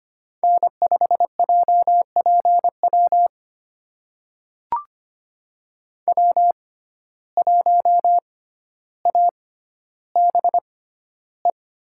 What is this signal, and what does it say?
Morse code, heard as a keyed tone, about 25 wpm, spaced as for 13 wpm, 700 Hz. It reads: N5JPW W1ABE
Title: USA Call Signs - 5 Characters - 6x Character Spacing - 25wpm